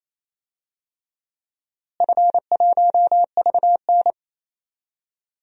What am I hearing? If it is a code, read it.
Morse code, heard as a keyed tone, about 27 wpm, 700 Hz. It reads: F1VN
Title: DXCC-100 Call Signs - 4 Characters - Repeated 3x - 28wpm